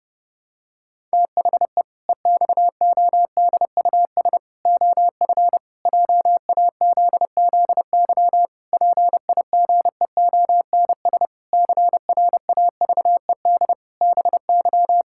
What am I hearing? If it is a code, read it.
Morse code, heard as a keyed tone, about 29 wpm, 700 Hz. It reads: THE EXODUS OF JAZZY PIGEONS CRAVED BY
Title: Pangram Challenge v2 - 30wpm